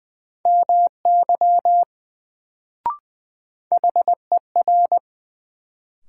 Morse code, heard as a keyed tone, about 20 wpm, 700 Hz. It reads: MY HER
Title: Sets of 2 Words - New Word Drill - 20wpm